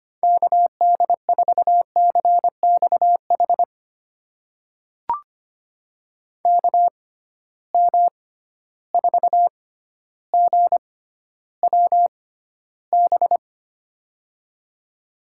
Morse code, heard as a keyed tone, about 25 wpm, spaced as for 13 wpm, 700 Hz. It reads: KD4CXH KM4GWB
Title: USA Call Signs - 6 Characters - 6x Character Spacing - 25wpm